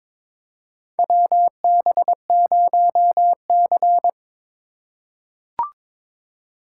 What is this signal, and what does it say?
Morse code, heard as a keyed tone, about 22 wpm, 700 Hz. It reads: WB0C
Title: USA Call Signs - 4 Characters - 3x Character Spacing - 22wpm